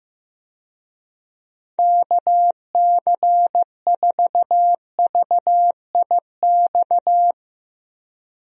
Morse code, heard as a keyed tone, about 15 wpm, 700 Hz. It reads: KC4VIX
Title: USA Call Signs - 6 Characters - 15wpm